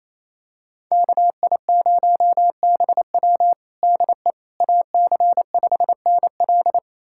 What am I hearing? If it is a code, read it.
Morse code, heard as a keyed tone, about 28 wpm, 700 Hz. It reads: KI0BW DE AC5NL
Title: USA Callsign DE Callsign - 5 Characters - 28wpm